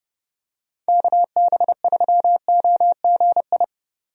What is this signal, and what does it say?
Morse code, heard as a keyed tone, about 32 wpm, 700 Hz. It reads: KB3OGI
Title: USA Call Signs - 6 Characters - 30wpm